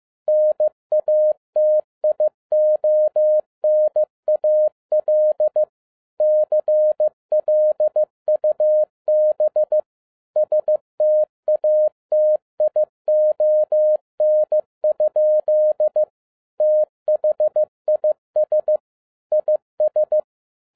Morse code, heard as a keyed tone, about 15 wpm, 610 Hz. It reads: NATIONAL CLUB STATION? THIS IS
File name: 2011-10-mp3 / n12-JL3YMV 15wpm